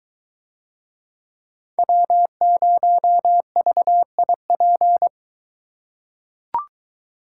Morse code, heard as a keyed tone, about 23 wpm, 700 Hz. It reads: W0VIP